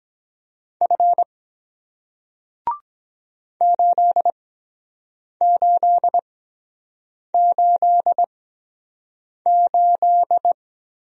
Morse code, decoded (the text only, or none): F 8888